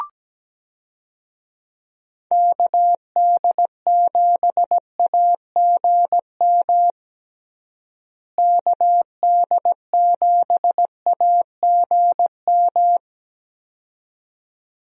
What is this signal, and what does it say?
Morse code, heard as a keyed tone, about 17 wpm, 700 Hz. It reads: KD7AGM KD7AGM
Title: USA Call Signs - 6 Characters - Repeated 2x - 17wpm